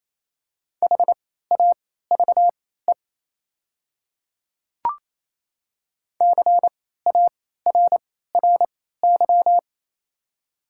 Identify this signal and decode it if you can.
Morse code, heard as a keyed tone, about 28 wpm, spaced as for 16 wpm, 700 Hz. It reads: HAVE CARRY
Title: 3x Character Spacing - Top 500 Words - 28wpm